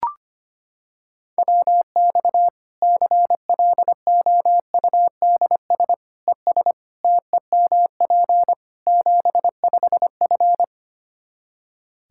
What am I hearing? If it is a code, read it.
Morse code, heard as a keyed tone, about 25 wpm, 700 Hz. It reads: WX CLOUDS ES TEMP 75F
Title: K1USN SST QSO Practice 25wpm